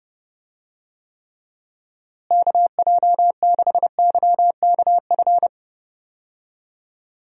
Morse code, heard as a keyed tone, about 32 wpm, 700 Hz. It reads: KJ6YKF
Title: USA Call Signs - 6 Characters - Repeated 2x - 30wpm